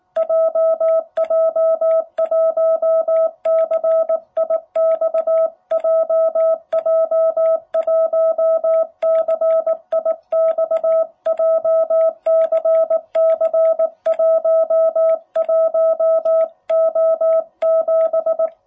Morse code, heard as a keyed tone, about 19 wpm, 660 Hz. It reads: JJ1CIX JJ1CIX JCC11O7
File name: jh8cbh / 2021cw